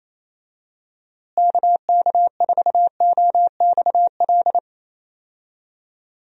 Morse code, heard as a keyed tone, about 27 wpm, 700 Hz. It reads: KK4OXL
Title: USA Call Signs - 6 Characters - Repeated 2x - 28wpm